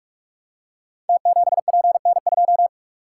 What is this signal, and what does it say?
Morse code, heard as a keyed tone, about 45 wpm, 700 Hz. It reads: T7PN2
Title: Code Groups - 5 Letter:Numbers - 45wpm